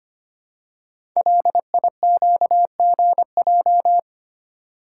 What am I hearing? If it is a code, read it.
Morse code, heard as a keyed tone, about 25 wpm, 700 Hz. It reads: LIQGJ
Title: Code Groups - 5 Letters - 25wpm